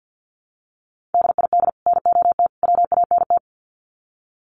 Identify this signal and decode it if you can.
Morse code, heard as a keyed tone, about 50 wpm, 700 Hz. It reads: DID NOT WANT
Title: Sets of 3 Words 50wpm